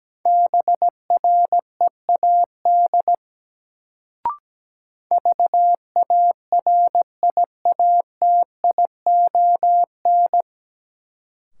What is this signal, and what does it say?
Morse code, heard as a keyed tone, about 17 wpm, 700 Hz. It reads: BREAD VARIATION
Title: Sentences from Top 1600 Words - New Word Drill - 17wpm